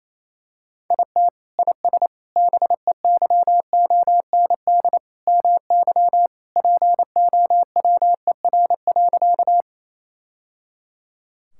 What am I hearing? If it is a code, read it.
Morse code, heard as a keyed tone, about 27 wpm, 700 Hz. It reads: IT IS BEYOND MY POWER.
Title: Sentences from Top 1000 Words - Review Part 6 - 28wpm